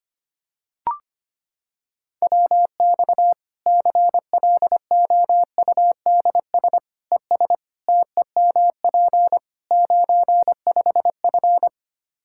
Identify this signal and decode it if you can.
Morse code, heard as a keyed tone, about 25 wpm, 700 Hz. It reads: WX CLOUDS ES TEMP 95F
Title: Practice QSO - Exchange 2 - Info - 25wpm